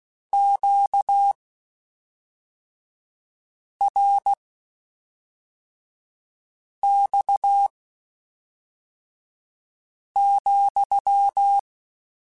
Morse code, heard as a keyed tone, about 16 wpm, spaced as for 3 wpm, 800 Hz. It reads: QRX,
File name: Q Calls / Q Calls-mp3s / Q Calls sppc @5